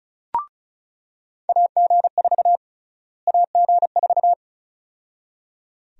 Morse code, heard as a keyed tone, about 36 wpm, 700 Hz. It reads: AG4 AG4
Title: USA Call Sign Prefixes - Repeated 2x - 35wpm